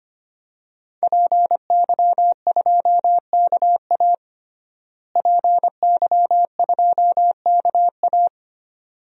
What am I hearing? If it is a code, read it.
Morse code, heard as a keyed tone, about 25 wpm, 700 Hz. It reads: PY2KA PY2KA
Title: DXCC-100 Call Signs - 5 Characters - Repeated 2x - 25wpm